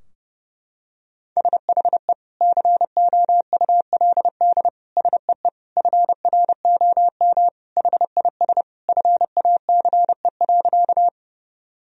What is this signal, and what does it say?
Morse code, heard as a keyed tone, about 30 wpm, 700 Hz. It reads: SHE COULD SEE FROM HIS FACE.
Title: Sentences from Top 1000 Words - Review Part 2 - 30wpm